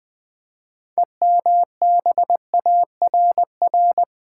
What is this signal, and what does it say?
Morse code, heard as a keyed tone, about 20 wpm, 700 Hz. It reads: EMBARR
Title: Build A Word - 6 Letters At A Time - 12-Letter Words - 20wpm